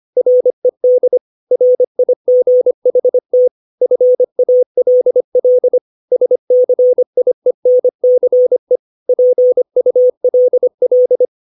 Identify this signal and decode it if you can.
Morse code, heard as a keyed tone, about 25 wpm, 500 Hz. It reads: RED RIGHT FALL SCIENCE PULL
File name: 632 words / 632 Words 25WPM